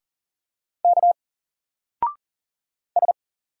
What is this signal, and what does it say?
Morse code, heard as a keyed tone, about 40 wpm, 700 Hz. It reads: K S
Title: Single Letter-Number 40wpm